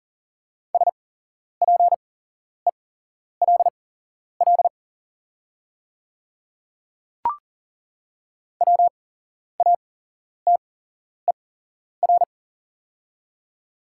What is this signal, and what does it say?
Morse code, heard as a keyed tone, about 40 wpm, spaced as for 11 wpm, 700 Hz. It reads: SPELL WATER
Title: 8x Character Spacing - Top 300 Words - 40wpm